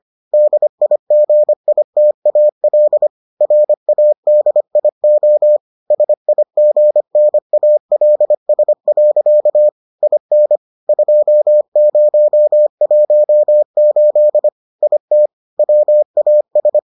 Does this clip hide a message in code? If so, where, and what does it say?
Yes: Morse code from 0.3 to 16.8 s, heard as a keyed tone, about 25 wpm, 600 Hz. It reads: DIGITAL RADIO SIGNALS. IN 2018 IT WAS